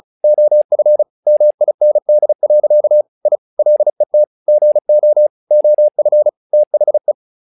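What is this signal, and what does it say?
Morse code, heard as a keyed tone, about 36 wpm, 600 Hz. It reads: OF MIND. I LET GO OF THE
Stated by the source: Morse practice text "Tower Trouble" sent at 35 WPM, from the SKCC - Straight Key Century Club learning center audio files.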